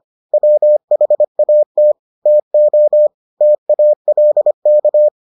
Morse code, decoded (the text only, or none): WHAT TO TALK